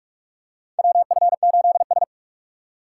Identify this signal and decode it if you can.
Morse code, heard as a keyed tone, about 46 wpm, 700 Hz. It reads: WF8S
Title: USA Call Signs - 4 Characters - 3x Character Spacing - 45wpm